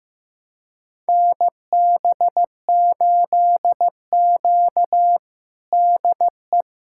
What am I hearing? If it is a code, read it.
Morse code, heard as a keyed tone, about 15 wpm, 700 Hz. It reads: NB8Q DE
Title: USA Callsign DE Callsign - 4 Characters - 15wpm